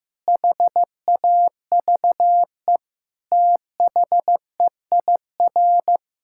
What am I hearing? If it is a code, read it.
Morse code, heard as a keyed tone, about 15 wpm, 700 Hz. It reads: HAVE THEIR
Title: Sentences from Top 1000 Words - Review Part 6 - 15wpm